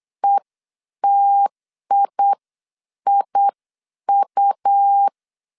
Morse code, heard as a keyed tone, about 8 wpm, spaced as for 7 wpm, 800 Hz. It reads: ETIIU
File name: CW / DET / 1 6 1500C S B 8wpm 5142